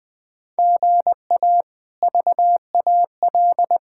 Morse code, heard as a keyed tone, about 20 wpm, 700 Hz. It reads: GA VAL
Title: K1USN SST QSO Practice - 20wpm